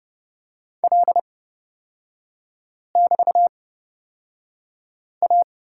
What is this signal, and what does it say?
Morse code, heard as a keyed tone, about 31 wpm, spaced as for 5 wpm, 700 Hz. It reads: L<BT>A
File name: Lesson 36e - BT - Three Characters 30wpm